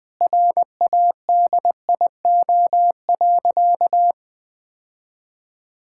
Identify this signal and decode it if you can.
Morse code, heard as a keyed tone, about 20 wpm, 700 Hz. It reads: RADIO.
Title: Sentences with 5-Word Phrases - 20wpm